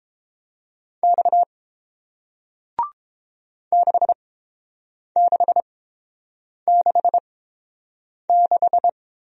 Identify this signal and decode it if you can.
Morse code, heard as a keyed tone, about 30 wpm, spaced as for 7 wpm, 700 Hz. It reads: X 6666